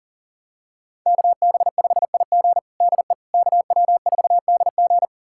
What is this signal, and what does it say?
Morse code, heard as a keyed tone, about 40 wpm, 700 Hz. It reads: KB5IG DE KW4DG
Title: USA Callsign DE Callsign - 5 Characters - 40wpm